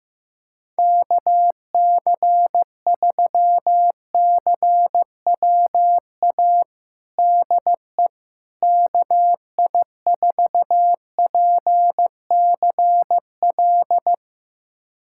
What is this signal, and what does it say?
Morse code, heard as a keyed tone, about 15 wpm, 700 Hz. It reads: KC3CWA DE KI4PCL
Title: USA Callsign DE Callsign 15wpm